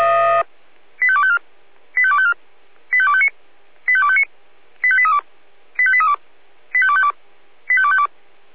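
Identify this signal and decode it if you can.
Five-tone selective call (ZVEI), heard as quick runs of five tones, tones of about 70 ms. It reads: 85254 85254 85259 85259 85832 85832 85252 85252
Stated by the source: Five-tone ZVEI selective call received on an SDR.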